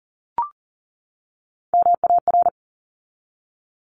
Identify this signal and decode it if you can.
Morse code, heard as a keyed tone, about 40 wpm, 700 Hz. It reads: MAR
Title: ARRL Sections 40wpm